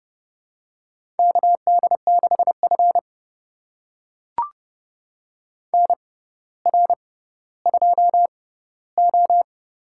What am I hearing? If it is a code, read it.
Morse code, heard as a keyed tone, about 30 wpm, spaced as for 15 wpm, 700 Hz. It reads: KD6F NR2O
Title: USA Call Signs - 4 Characters - 6x Character Spacing - 30wpm